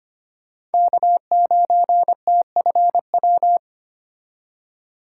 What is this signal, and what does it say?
Morse code, heard as a keyed tone, about 25 wpm, 690 Hz. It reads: K9TFW
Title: USA Call Signs - 5 Characters - 6x Character Spacing - 25wpm